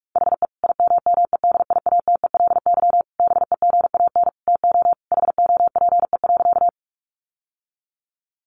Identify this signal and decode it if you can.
Morse code, heard as a keyed tone, about 44 wpm, 700 Hz. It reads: HE IMMEDIATELY BEGAN TO HOPE.
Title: Sentences from Top 1100 Words 45wpm